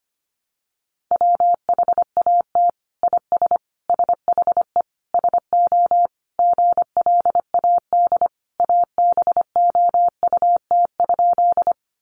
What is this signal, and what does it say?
Morse code, heard as a keyed tone, about 25 wpm, 700 Hz. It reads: WHAT IS SHE SO GLAD ABOUT?
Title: Sentences from Top 800 Words 25wpm